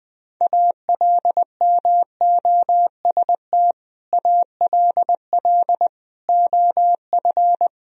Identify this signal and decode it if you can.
Morse code, heard as a keyed tone, about 20 wpm, 700 Hz. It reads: ALMOST ALL OF
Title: Sentences from Top 1000 Words - Review Part 4 - 20wpm